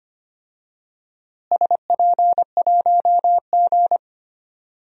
Morse code, heard as a keyed tone, about 25 wpm, 700 Hz. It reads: SP1G
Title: DXCC-100 Call Signs - 4 Characters - Repeated 2x - 25wpm